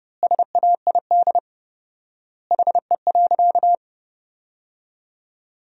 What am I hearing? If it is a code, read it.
Morse code, heard as a keyed tone, about 30 wpm, 700 Hz. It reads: SAID HE.
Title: 4x Word-Spacing - Sentences from Top 100 Words - 30wpm